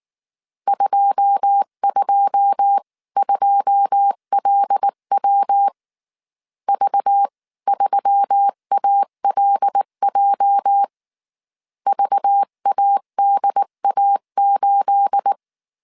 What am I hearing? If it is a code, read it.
Morse code, heard as a keyed tone, about 19 wpm, 800 Hz. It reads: 222LW V3ALJ VADA8
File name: CW / DET / 1 9 1500C S B 18wpm 2551